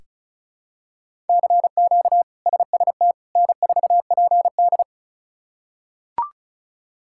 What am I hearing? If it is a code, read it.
Morse code, heard as a keyed tone, about 36 wpm, 700 Hz. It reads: CQ SST N4PD
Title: K1USN SST QSO Practice - 35wpm